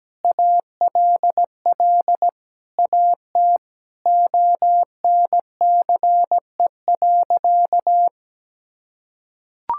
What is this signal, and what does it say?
Morse code, heard as a keyed tone, about 17 wpm, 700 Hz. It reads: ALL AT ONCE.
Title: Sentences from Top 1000 Words - Review Part 5 - 17wpm